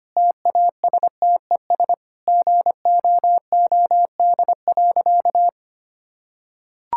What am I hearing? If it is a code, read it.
Morse code, heard as a keyed tone, about 25 wpm, 700 Hz. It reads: TASTES GOOD.
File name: Easy Sentences with Hints - 25wpm